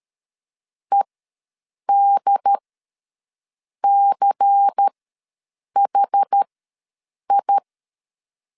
Morse code, heard as a keyed tone, about 13 wpm, spaced as for 7 wpm, 800 Hz. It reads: EDCHI
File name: LEZIONI / 5 LBVC / 5 LBVC ws 12wpm 3159